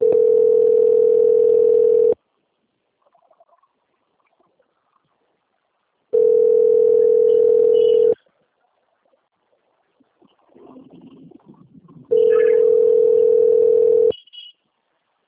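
Ringback tone (North America), 440 + 480 Hz.
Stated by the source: Outgoing telephone call, Presidente Prudente, São Paulo, Brazil.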